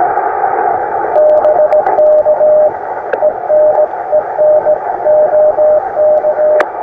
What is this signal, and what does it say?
Morse code, heard as a keyed tone, about 16 wpm, 610 Hz, with noise. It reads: BK ENEN OK
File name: attachments / iz0aym 20m